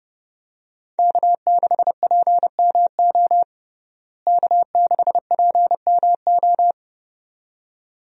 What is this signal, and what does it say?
Morse code, heard as a keyed tone, about 30 wpm, 700 Hz. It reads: K6PMO K6PMO
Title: USA Call Signs - 5 Characters - Repeated 2x - 30wpm